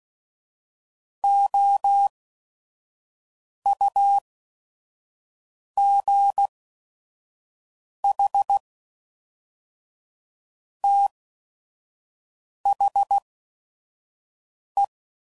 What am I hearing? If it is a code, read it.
Morse code, heard as a keyed tone, about 16 wpm, spaced as for 5 wpm, 800 Hz. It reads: OUGH THE
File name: CW Code Groups @5WPM / Technical Subjects / Reactance